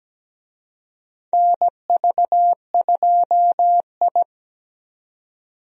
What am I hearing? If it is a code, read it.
Morse code, heard as a keyed tone, about 17 wpm, 700 Hz. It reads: NV2I
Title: USA Call Signs - 4 Characters - 17wpm